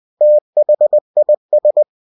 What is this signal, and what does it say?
Morse code, heard as a keyed tone, about 20 wpm, 600 Hz. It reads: THIS
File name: cw-books / winnie-the-pooh / winnie-20wpm-page-0014